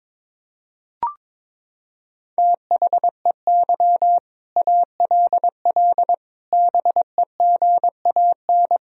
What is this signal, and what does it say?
Morse code, heard as a keyed tone, about 22 wpm, 700 Hz. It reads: THEY ALL BEGAN
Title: Sentences from Top 1000 Words - Review Part 4 - 22wpm